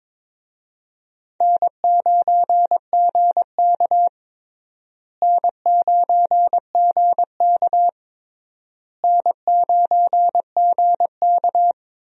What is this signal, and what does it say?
Morse code, heard as a keyed tone, about 22 wpm, 700 Hz. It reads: N9GK N9GK N9GK
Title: USA Call Signs - 4 Characters - Repeated 3x - 22wpm